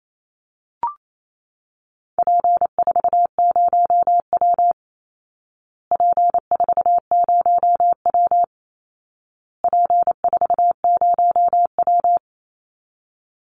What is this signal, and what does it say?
Morse code, heard as a keyed tone, about 28 wpm, 700 Hz. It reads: P40W P40W P40W
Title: Call Signs Repeated 2x - 28wpm